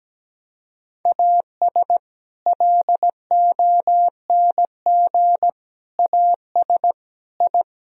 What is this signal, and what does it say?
Morse code, heard as a keyed tone, about 17 wpm, 700 Hz. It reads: AS LONG AS I
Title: Sentences from Top 1000 Words - Review Part 3 - 17wpm